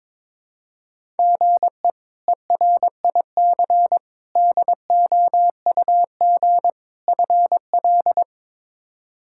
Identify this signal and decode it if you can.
Morse code, heard as a keyed tone, about 21 wpm, 700 Hz. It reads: GE ERIC DOUG FL
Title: K1USN SST QSO Practice - 22wpm